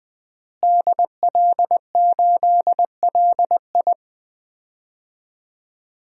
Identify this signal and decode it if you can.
Morse code, heard as a keyed tone, about 19 wpm, 700 Hz. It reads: DL8LI
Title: DXCC-100 Call Signs - 5 Characters - Repeated 3x - 20wpm